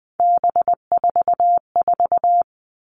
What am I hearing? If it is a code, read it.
Morse code, heard as a keyed tone, about 20 wpm, 700 Hz. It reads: B44
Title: All Cut Numbers - 3-Digit 20wpm